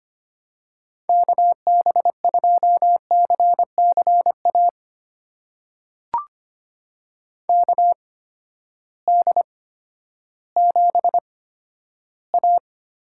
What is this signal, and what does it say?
Morse code, heard as a keyed tone, about 25 wpm, spaced as for 12 wpm, 700 Hz. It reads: KB2CCA KD7A
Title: USA Call Signs - 8x Character Spacing - 25wpm